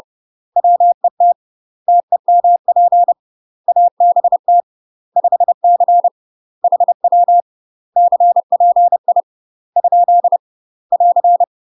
Morse code, heard as a keyed tone, about 30 wpm, 700 Hz. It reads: WET TEMP ABT 5C HW CPI ? <AR>
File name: UK MASTER QSOs TRAINING / 30WPM / UK MASTER QSOs 0050